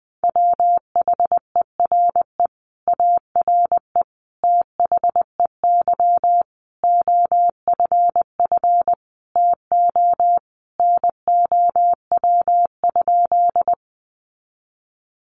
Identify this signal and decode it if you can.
Morse code, heard as a keyed tone, about 20 wpm, 700 Hz. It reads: WHERE ARE THEY OFF TO NOW?
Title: Sentences from Top 200 Words 20wpm